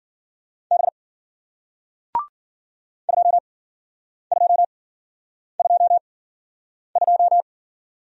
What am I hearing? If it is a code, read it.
Morse code, heard as a keyed tone, about 52 wpm, spaced as for 9 wpm, 700 Hz. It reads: B 2222